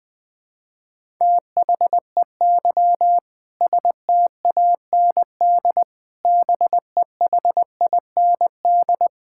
Morse code, heard as a keyed tone, about 20 wpm, 700 Hz. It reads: THEY STAND BEHIND
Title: Sentences from Top 1000 Words - Review Part 5 - 20wpm